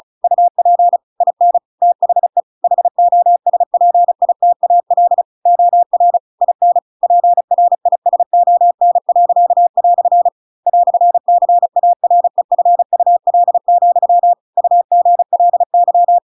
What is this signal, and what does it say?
Morse code, heard as a keyed tone, about 34 wpm, 700 Hz. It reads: UP IN THE HOSPITAL OR IN PRISON." "CAREFUL, UGLY